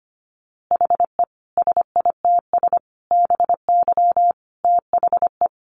Morse code, heard as a keyed tone, about 25 wpm, 700 Hz. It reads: HE SITS BY THE
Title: Easy Phrases 25wpm